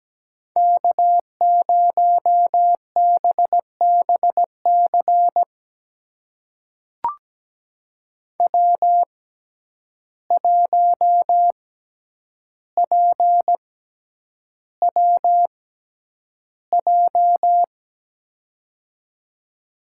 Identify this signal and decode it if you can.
Morse code, heard as a keyed tone, about 17 wpm, spaced as for 9 wpm, 700 Hz. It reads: K0BBC W1PWJ
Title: USA Call Signs - 5 Characters - 6x Character Spacing - 17wpm